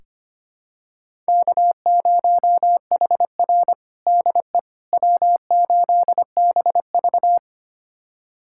Morse code, heard as a keyed tone, about 25 wpm, 700 Hz. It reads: K0HR DE W8BV
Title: USA Callsign DE Callsign - 4 Characters - 25wpm